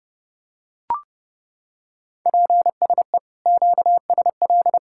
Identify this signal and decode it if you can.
Morse code, heard as a keyed tone, about 30 wpm, 700 Hz. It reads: PSE QSL